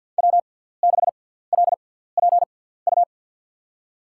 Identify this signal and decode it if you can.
Morse code, heard as a keyed tone, about 52 wpm, spaced as for 17 wpm, 700 Hz. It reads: W6LPU
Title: USA Call Signs - 5 Characters - 6x Character Spacing - 50wpm